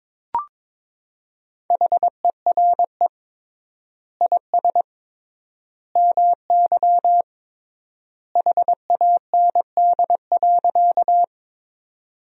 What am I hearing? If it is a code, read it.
Morse code, heard as a keyed tone, about 21 wpm, 700 Hz. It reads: HERE IS MY HAND.
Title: 3x Word-Spacing - Sentences from Top 200 Words - 22wpm